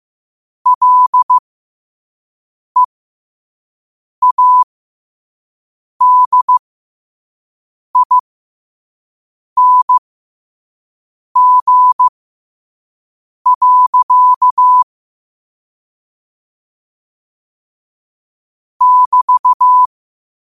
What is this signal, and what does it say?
Morse code, heard as a keyed tone, about 15 wpm, spaced as for 5 wpm, 1000 Hz. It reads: LEADING. <BT>
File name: qotdfiles / 5wpm